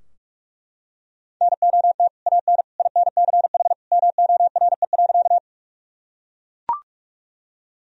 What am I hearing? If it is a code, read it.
Morse code, heard as a keyed tone, about 41 wpm, 700 Hz. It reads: NOT AN INCH MORE.
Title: Sentences from Top 1000 Words - Review Part 3 - 45wpm